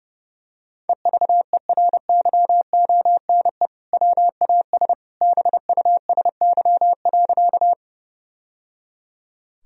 Morse code, heard as a keyed tone, about 30 wpm, 700 Hz. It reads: EVERYONE WAS BUSY.
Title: Sentences from Top 1000 Words - Review Part 3 - 30wpm